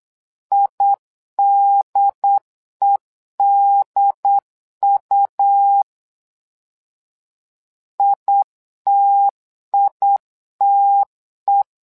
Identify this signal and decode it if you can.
Morse code, heard as a keyed tone, about 8 wpm, 800 Hz. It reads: IDEDU ITITE
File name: LEZIONI / 2 UHDN / 2 UHD 8wpm 2258